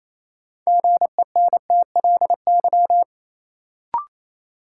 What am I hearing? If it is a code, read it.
Morse code, heard as a keyed tone, about 28 wpm, 700 Hz. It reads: GENTLY
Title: Sentences from Top 1500 Words - New Word Drill Alt - 28wpm